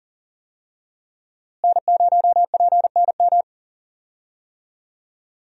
Morse code, heard as a keyed tone, about 40 wpm, 700 Hz. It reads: N0PNM
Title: USA Call Signs - 5 Characters - Repeated 2x - 40wpm